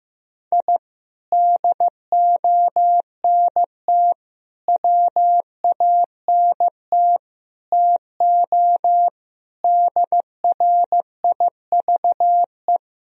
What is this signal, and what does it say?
Morse code, heard as a keyed tone, about 16 wpm, 700 Hz. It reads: I DONT WANT TO DRIVE
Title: Sentences from Top 1000 Words - Review Part 2 - 15wpm